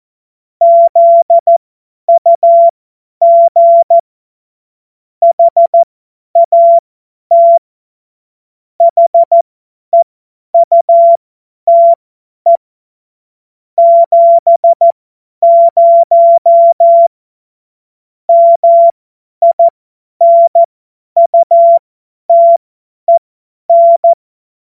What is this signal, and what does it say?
Morse code, heard as a keyed tone, about 14 wpm, spaced as for 10 wpm, 680 Hz. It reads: ZUG HAT HEUTE 70 MINUTEN
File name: Morsekurs / Witze / Witz 20 Februar 2024 50 F70